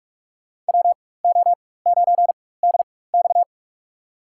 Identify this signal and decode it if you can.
Morse code, heard as a keyed tone, about 46 wpm, spaced as for 21 wpm, 700 Hz. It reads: WO9DX
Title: USA Call Signs - 5 Characters - 4x Character Spacing - 45wpm